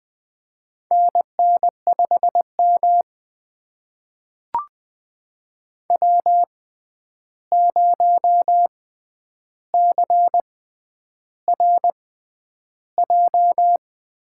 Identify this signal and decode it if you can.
Morse code, heard as a keyed tone, about 20 wpm, spaced as for 10 wpm, 700 Hz. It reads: NN5M W0CRJ
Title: USA Call Signs - 6x Character Spacing - 20wpm